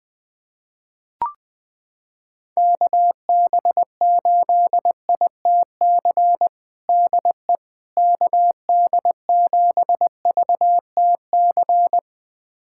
Morse code, heard as a keyed tone, about 21 wpm, 700 Hz. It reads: KB8ITC DE KD7VTC
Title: USA Callsign DE Callsign - 6 Characters - 20wpm